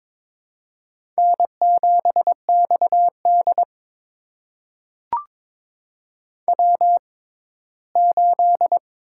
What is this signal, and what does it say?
Morse code, heard as a keyed tone, about 22 wpm, spaced as for 15 wpm, 700 Hz. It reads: N7XD W8
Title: USA Call Signs - 4 Characters - 6x Character Spacing - 22wpm